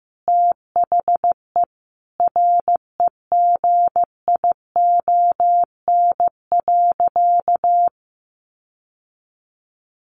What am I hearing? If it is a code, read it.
Morse code, heard as a keyed tone, about 15 wpm, 700 Hz. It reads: THE REGION.
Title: Sentences from Top 1500 Words 15wpm